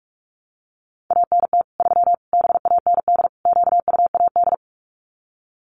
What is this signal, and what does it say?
Morse code, heard as a keyed tone, about 44 wpm, 700 Hz. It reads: ANT 3 BAND QUAD
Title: Common QSO Phrases 45wpm